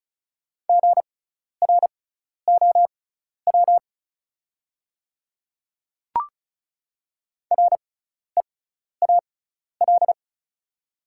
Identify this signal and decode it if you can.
Morse code, heard as a keyed tone, about 36 wpm, spaced as for 12 wpm, 690 Hz. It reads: GROW REAL
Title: 6x Character Spacing - Top 300 Words - 35wpm